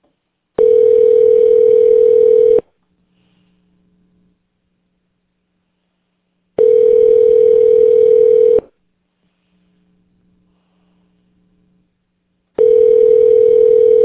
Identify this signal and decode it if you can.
Ringback tone (North America), 440 + 480 Hz.